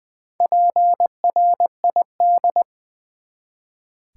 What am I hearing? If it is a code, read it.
Morse code, heard as a keyed tone, about 20 wpm, 700 Hz. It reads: PRID
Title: Build A Word - 1 Letter At A Time - 5-Letter Words - 20wpm